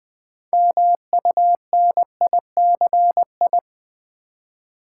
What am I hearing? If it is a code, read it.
Morse code, heard as a keyed tone, about 20 wpm, 700 Hz. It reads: MUNICI
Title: Build A Word - 3 Letters At A Time - 9-Letter Words - 20wpm